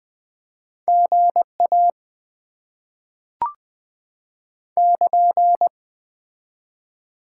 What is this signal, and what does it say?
Morse code, heard as a keyed tone, about 20 wpm, 700 Hz. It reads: GA <KN>
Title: Top 90 QSO Elements - 2x Character Spacing - 20wpm